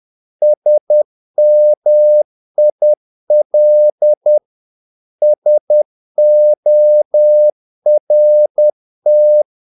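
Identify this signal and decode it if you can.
Morse code, heard as a keyed tone, about 10 wpm, 600 Hz. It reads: SMIL SORT
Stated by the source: Morse practice of short Danish words at 10 wpm, from CW med Gnister.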